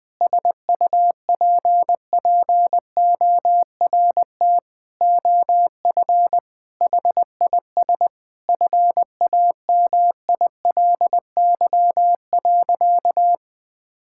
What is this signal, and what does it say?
Morse code, heard as a keyed tone, about 21 wpm, 700 Hz. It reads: SUPPORT OF HIS FAMILY.
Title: Sentences from Top 1000 Words - Review Part 5 - 20wpm